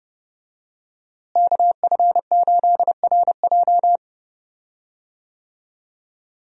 Morse code, heard as a keyed tone, about 30 wpm, 700 Hz. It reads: KF8RJ
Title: USA Call Signs - 5 Characters - Repeated 3x - 30wpm